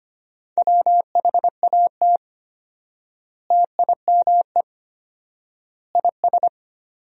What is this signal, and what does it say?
Morse code, heard as a keyed tone, about 25 wpm, 700 Hz. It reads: WHAT TIME IS
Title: 4x Word-Spacing - Sentences from Top 100 Words - 25wpm